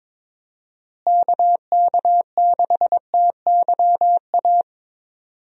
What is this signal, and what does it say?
Morse code, heard as a keyed tone, about 22 wpm, 700 Hz. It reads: KK6TYA
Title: USA Call Signs - 6 Characters - Repeated 2x - 22wpm